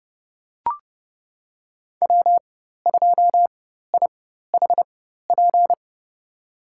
Morse code, heard as a keyed tone, about 30 wpm, spaced as for 14 wpm, 700 Hz. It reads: W2IHP
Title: USA Call Signs - 5 Characters - 4x Character Spacing - 30wpm